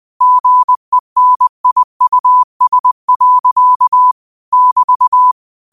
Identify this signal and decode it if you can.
Morse code, heard as a keyed tone, about 20 wpm, 1000 Hz. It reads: GENIUS. <BT>
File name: qotdfiles / 20wpm